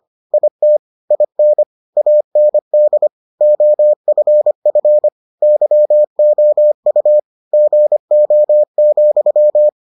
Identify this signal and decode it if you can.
Morse code, heard as a keyed tone, about 25 wpm, 600 Hz. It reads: IT IN AND OFF YOU GO,